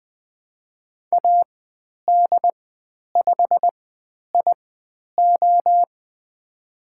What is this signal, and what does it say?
Morse code, heard as a keyed tone, about 20 wpm, spaced as for 10 wpm, 700 Hz. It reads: AD5IO
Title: 095 Lesson 16g - Letter c - Call Signs 20wpm